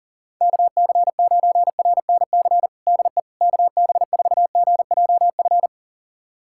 Morse code, heard as a keyed tone, about 40 wpm, 700 Hz. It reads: KC9RNC DE KB4GJF
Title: USA Callsign DE Callsign 40wpm